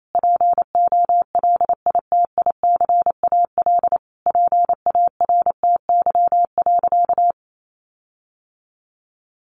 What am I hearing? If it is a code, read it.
Morse code, heard as a keyed tone, about 28 wpm, 700 Hz. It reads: POLITICAL PARTY.
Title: Sentences from Top 1400 Words 28wpm